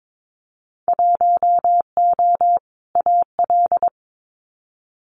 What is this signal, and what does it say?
Morse code, heard as a keyed tone, about 22 wpm, 700 Hz. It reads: 1O AL
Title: Winter Field Day Exchanges 22wpm